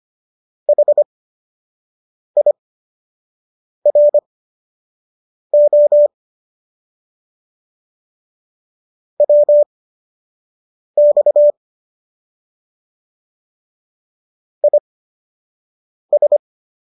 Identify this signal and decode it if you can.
Morse code, heard as a keyed tone, about 25 wpm, spaced as for 6 wpm, 590 Hz. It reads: HIRO WX IS